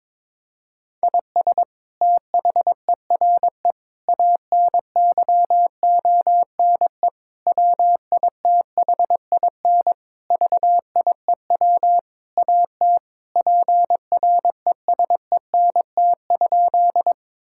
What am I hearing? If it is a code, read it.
Morse code, heard as a keyed tone, about 22 wpm, 700 Hz. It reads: IS THERE ANYONE WITHIN VIEW AT PRESENT?
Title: Sentences from Top 1000 Words - Review Part 6 - 22wpm